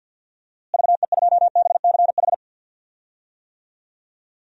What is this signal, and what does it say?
Morse code, heard as a keyed tone, about 50 wpm, 700 Hz. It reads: VE2BCH
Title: DXCC-100 Call Signs - 6 Characters - Repeated 2x - 50wpm